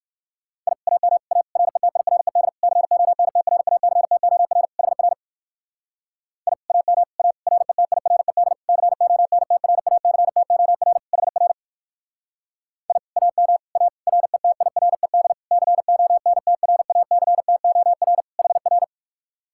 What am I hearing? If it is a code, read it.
Morse code, heard as a keyed tone, about 59 wpm, 700 Hz. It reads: I AM A RETIRED CONTRACTOR HR I AM A RETIRED CONTRACTOR HR I AM A RETIRED CONTRACTOR HR